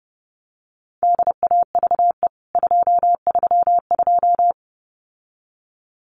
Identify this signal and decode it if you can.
Morse code, heard as a keyed tone, about 30 wpm, 700 Hz. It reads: DAVE 232
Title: CWT Exchange 30wpm